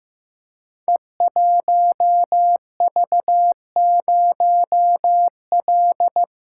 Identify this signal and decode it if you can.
Morse code, heard as a keyed tone, about 15 wpm, 700 Hz. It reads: E1V0L
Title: Code Groups - 5 Letter:Numbers - 15wpm